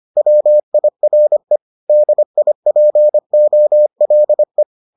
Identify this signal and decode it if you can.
Morse code, heard as a keyed tone, about 25 wpm, 600 Hz. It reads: WIRE DIPOLE